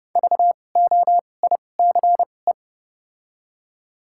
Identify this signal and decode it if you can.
Morse code, heard as a keyed tone, about 31 wpm, spaced as for 22 wpm, 700 Hz. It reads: VOICE
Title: 2x Character Spacing - Top 500 Words - 30wpm